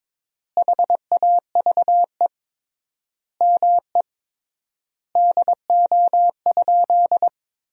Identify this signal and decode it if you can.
Morse code, heard as a keyed tone, about 22 wpm, 700 Hz. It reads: HAVE ME DO?
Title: 3x Word-Spacing - Sentences from Top 200 Words - 22wpm